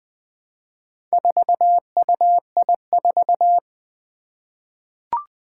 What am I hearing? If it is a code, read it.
Morse code, heard as a keyed tone, about 20 wpm, 700 Hz. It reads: 4UI4
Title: Four Challenging Letter-Number Combinations 20wpm